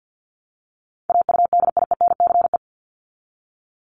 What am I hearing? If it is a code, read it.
Morse code, heard as a keyed tone, about 50 wpm, 700 Hz. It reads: AUDIENCE
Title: Top 751-1000 Words 50wpm